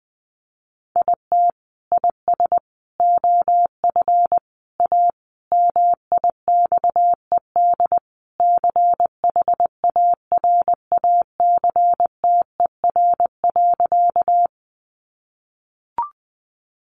Morse code, heard as a keyed tone, about 20 wpm, 700 Hz. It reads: IT IS OF A MIXED CHARACTER.
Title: Sentences from Top 1500 Words 20wpm